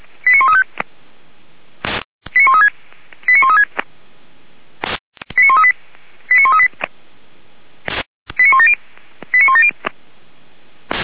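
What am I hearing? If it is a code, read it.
Five-tone selective call (ZVEI), heard as quick runs of five tones, tones of about 70 ms. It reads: 89135 89136 89136 89138 89138 89170 89170
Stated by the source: Five-tone ZVEI selective call received on an SDR.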